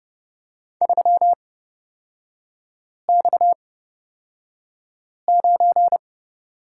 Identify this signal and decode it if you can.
Morse code, heard as a keyed tone, about 29 wpm, spaced as for 5 wpm, 700 Hz. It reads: 3X9
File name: Lesson 37e - Letter x - Three Characters 30wpm